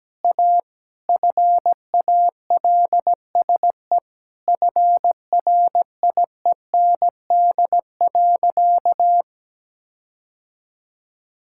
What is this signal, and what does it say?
Morse code, heard as a keyed tone, about 17 wpm, 700 Hz. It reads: A FALSE FRIEND.
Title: Sentences from Top 1600 Words 17wpm